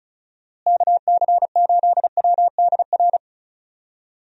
Morse code, heard as a keyed tone, about 36 wpm, 700 Hz. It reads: KC8WDR
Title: USA Call Signs - 6 Characters - Repeated 2x - 35wpm